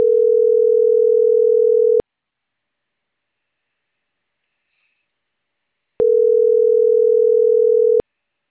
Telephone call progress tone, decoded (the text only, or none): ringback tone